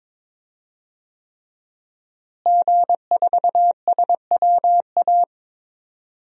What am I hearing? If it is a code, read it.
Morse code, heard as a keyed tone, about 22 wpm, 700 Hz. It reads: G4SWA
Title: DXCC-100 Call Signs - 5 Characters - Repeated 3x - 22wpm